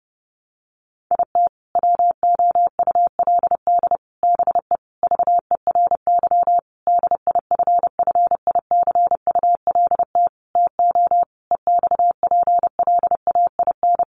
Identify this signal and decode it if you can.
Morse code, heard as a keyed tone, about 30 wpm, 700 Hz. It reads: IT WOULD BE VERY DIFFICULT TO EXPLAIN
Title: Sentences from Top 1100 Words 30wpm